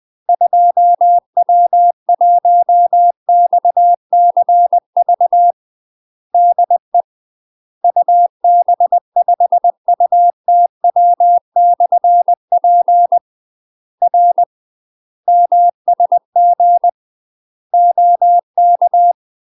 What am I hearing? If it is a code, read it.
Morse code, heard as a keyed tone, about 20 wpm, 700 Hz. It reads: 2W1XCV DE UB5UTW/P R MSG OK
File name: UK MASTER QSOs TRAINING / 20WPM / UK MASTER QSOs 0020